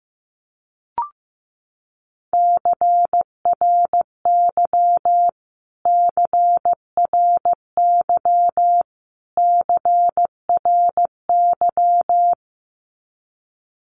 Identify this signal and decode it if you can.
Morse code, heard as a keyed tone, about 15 wpm, 700 Hz. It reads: CRY CRY CRY
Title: Top 500 Words Repeated 15wpm